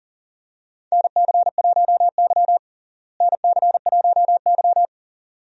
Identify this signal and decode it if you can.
Morse code, heard as a keyed tone, about 37 wpm, 700 Hz. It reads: NC1Y NC1Y
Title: USA Call Signs - 4 Characters - Repeated 2x - 40wpm